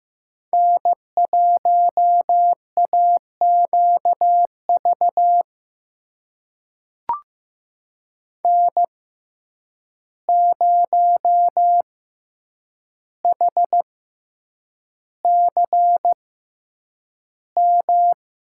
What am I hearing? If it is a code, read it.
Morse code, heard as a keyed tone, about 15 wpm, spaced as for 8 wpm, 700 Hz. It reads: N1AQV N0HCM
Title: USA Call Signs - 5 Characters - 6x Character Spacing - 15wpm